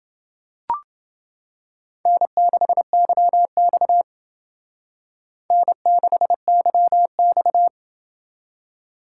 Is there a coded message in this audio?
Yes: Morse code from 2.1 to 7.7 s, heard as a keyed tone, about 28 wpm, 700 Hz. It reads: N6YX N6YX